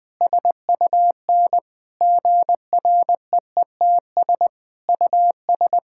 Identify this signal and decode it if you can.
Morse code, heard as a keyed tone, about 20 wpm, 700 Hz. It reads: SUN GREETS US
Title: Difficult Sentences with Hints - 20wpm